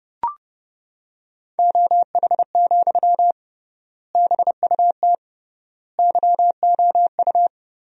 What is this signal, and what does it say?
Morse code, heard as a keyed tone, about 30 wpm, 700 Hz. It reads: OH, BUT YOU
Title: 3x Word-Spacing - Sentences from Top 500 Words - 30wpm